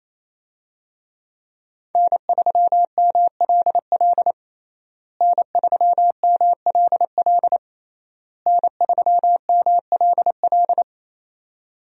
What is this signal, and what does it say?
Morse code, heard as a keyed tone, about 28 wpm, 700 Hz. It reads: N3MLL N3MLL N3MLL
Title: USA Call Signs - 5 Characters - Repeated 3x - 28wpm